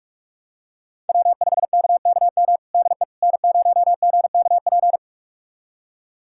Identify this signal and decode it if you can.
Morse code, heard as a keyed tone, about 46 wpm, 700 Hz. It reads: W5KKM DE N0GKP
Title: USA Callsign DE Callsign - 5 Characters - 45wpm